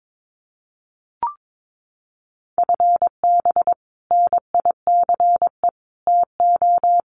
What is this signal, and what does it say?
Morse code, heard as a keyed tone, about 22 wpm, 700 Hz. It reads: FB NICE TO
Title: Common QSO Phrases 22wpm